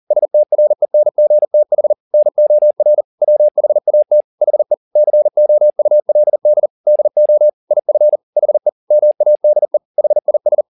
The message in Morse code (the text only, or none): STRENGTH NOR WHAT HE COULD DO IF HE MADE HIS